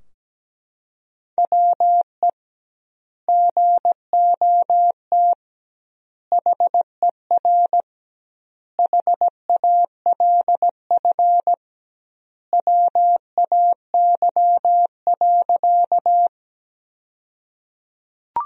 Morse code, heard as a keyed tone, about 17 wpm, 700 Hz. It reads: WE GOT HER HALF WAY.
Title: 2x Word-Spacing - Sentences from Top 400 Words - 17wpm